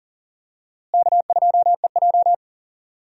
Morse code, heard as a keyed tone, about 40 wpm, 700 Hz. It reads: K2EJ
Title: USA Call Signs - 4 Characters - Repeated 3x - 40wpm